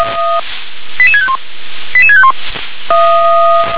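Five-tone selective call (ZVEI), heard as quick runs of five tones, tones of about 80 ms. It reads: 88651 88651
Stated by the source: Five-tone ZVEI selective call received on an SDR.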